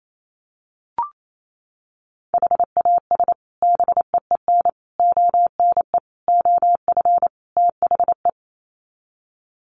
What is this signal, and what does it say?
Morse code, heard as a keyed tone, about 28 wpm, 700 Hz. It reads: HAS BEEN ONE OF THE
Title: Sets of 5 Words Encore 28wpm